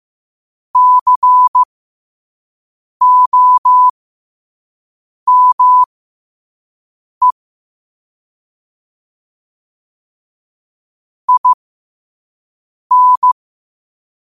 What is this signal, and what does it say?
Morse code, heard as a keyed tone, about 15 wpm, spaced as for 5 wpm, 1000 Hz. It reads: COME IN
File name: qotdfiles / 5wpm